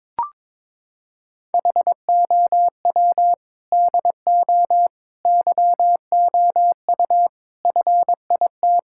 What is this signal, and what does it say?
Morse code, heard as a keyed tone, about 22 wpm, 700 Hz. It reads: HOW DO YOU FIT
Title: Sentences from Top 1000 Words - Review Part 6 - 22wpm